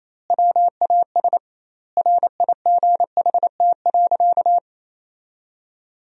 Morse code, heard as a keyed tone, about 28 wpm, 700 Hz. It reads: WAS RIGHT.
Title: 2x Word-Spacing - Sentences from Top 500 Words - 28wpm